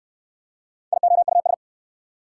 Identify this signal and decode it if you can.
Morse code, heard as a keyed tone, about 70 wpm, 690 Hz. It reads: IOUS